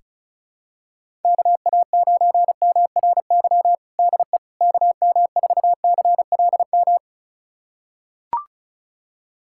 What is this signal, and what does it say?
Morse code, heard as a keyed tone, about 36 wpm, 700 Hz. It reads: KA9MRY DE KM4CLM
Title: USA Callsign DE Callsign - 6 Characters - 35wpm